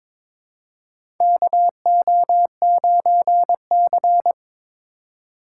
Morse code, heard as a keyed tone, about 22 wpm, 700 Hz. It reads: KO9C
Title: USA Call Signs - 4 Characters - 3x Character Spacing - 22wpm